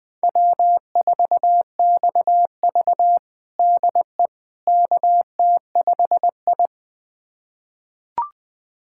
Morse code, heard as a keyed tone, about 20 wpm, 700 Hz. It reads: W4XV DE KT5I
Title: USA Callsign DE Callsign - 4 Characters - 20wpm